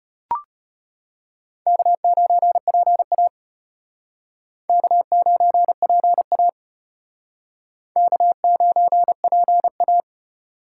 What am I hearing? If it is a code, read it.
Morse code, heard as a keyed tone, about 34 wpm, 700 Hz. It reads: K9PA K9PA K9PA